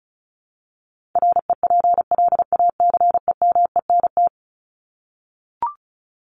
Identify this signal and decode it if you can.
Morse code, heard as a keyed tone, about 34 wpm, 700 Hz. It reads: REPLACEMENT
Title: Top 1751-2000 Words 35wpm